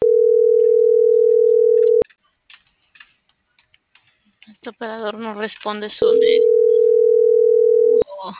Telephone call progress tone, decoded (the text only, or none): ringback tone